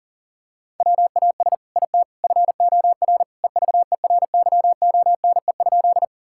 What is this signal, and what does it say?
Morse code, heard as a keyed tone, about 40 wpm, 700 Hz. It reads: WAS IT FOR EVERYONE?
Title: Sentences from Top 1000 Words - Review Part 5 - 40wpm